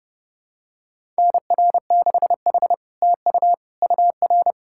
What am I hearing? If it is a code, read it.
Morse code, heard as a keyed tone, about 31 wpm, 700 Hz. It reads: NR6H TU UR
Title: SOTA QSO Practice 30wpm